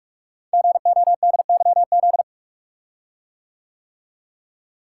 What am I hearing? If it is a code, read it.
Morse code, heard as a keyed tone, about 44 wpm, 690 Hz. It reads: GODYZ
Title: Code Groups - 5 Letters - 45wpm